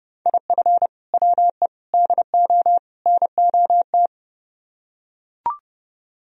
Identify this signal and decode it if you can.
Morse code, heard as a keyed tone, about 30 wpm, 700 Hz. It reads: IF WE DO NOT